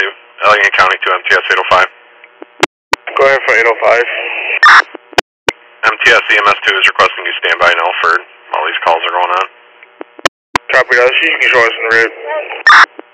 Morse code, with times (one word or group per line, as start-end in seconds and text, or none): none